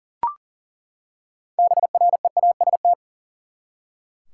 Morse code, heard as a keyed tone, about 40 wpm, 700 Hz. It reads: BREAST
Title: Six Letter Words - 40wpm